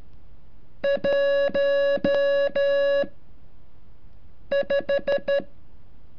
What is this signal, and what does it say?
Morse code, heard as a keyed tone, about 13 wpm, 560 Hz. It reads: 1 5
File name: curso / ejercicio 33